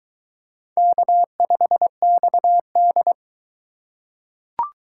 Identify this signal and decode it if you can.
Morse code, heard as a keyed tone, about 22 wpm, 700 Hz. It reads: K5XD